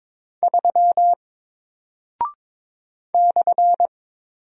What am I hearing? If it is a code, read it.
Morse code, heard as a keyed tone, about 22 wpm, 700 Hz. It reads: 3 /
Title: Single Character (Letter:Number:Punctuation:Prosign) Alt 22wpm